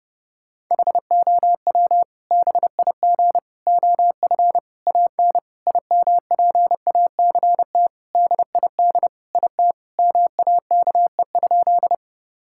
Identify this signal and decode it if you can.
Morse code, heard as a keyed tone, about 30 wpm, 700 Hz. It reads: HOW BIG OF AN IMPACT DID IT MAKE?
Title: Sentences from Top 1000 Words - Review Part 6 - 30wpm